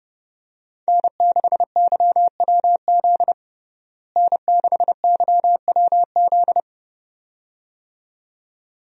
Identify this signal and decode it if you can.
Morse code, heard as a keyed tone, about 30 wpm, 700 Hz. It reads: N6YWZ N6YWZ
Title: USA Call Signs - 5 Characters - Repeated 2x - 30wpm